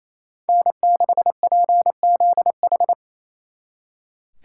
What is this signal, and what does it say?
Morse code, heard as a keyed tone, about 33 wpm, 700 Hz. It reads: N6PZH
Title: Speed Racing - USA 1x3 Call Signs 28wpm